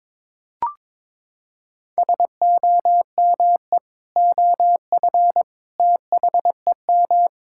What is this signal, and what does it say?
Morse code, heard as a keyed tone, about 22 wpm, 700 Hz. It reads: SOME OF THEM
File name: practice / 22wpm / Sentences with 3-Word Phrases - 22wpm